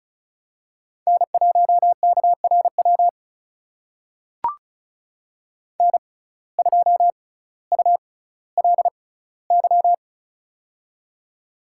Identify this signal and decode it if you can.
Morse code, heard as a keyed tone, about 36 wpm, spaced as for 18 wpm, 700 Hz. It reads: N1KRW N2ULY
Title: USA Call Signs - 5 Characters - 6x Character Spacing - 35wpm